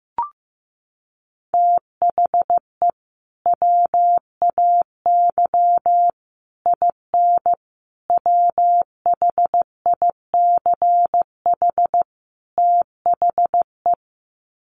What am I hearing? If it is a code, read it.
Morse code, heard as a keyed tone, about 15 wpm, 700 Hz. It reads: THE WAY IN WHICH THE
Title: Sets of 5 Words Encore2 15wpm